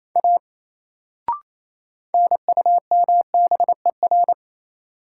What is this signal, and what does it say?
Morse code, heard as a keyed tone, about 28 wpm, 700 Hz. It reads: A NUMBER
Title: Sentences from Top 100 Words - New Word Drill - 28wpm